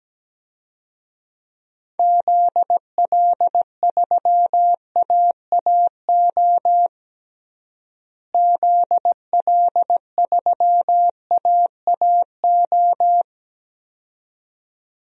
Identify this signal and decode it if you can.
Morse code, heard as a keyed tone, about 17 wpm, 700 Hz. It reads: ZL3AAO ZL3AAO
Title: DXCC-100 Call Signs - 6 Characters - Repeated 2x - 17wpm